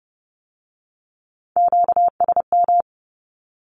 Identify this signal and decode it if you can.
Morse code, heard as a keyed tone, about 30 wpm, 700 Hz. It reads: QSM
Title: Q-Codes with Spoken Meaning - 30wpm